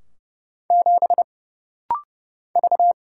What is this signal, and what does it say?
Morse code, heard as a keyed tone, about 30 wpm, 700 Hz. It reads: 7 V
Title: Single Letter-Number - Rapid-Fire 30wpm